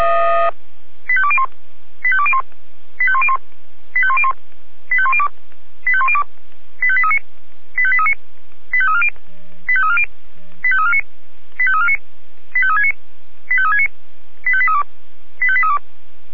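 Five-tone selective call (ZVEI), heard as quick runs of five tones, tones of about 70 ms. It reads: 85201 85201 85101 85101 85102 85102 85839 85839 85340 85340 85349 85349 85369 85369 85832 85832
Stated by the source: ZVEI five-tone selective call received on an SDR.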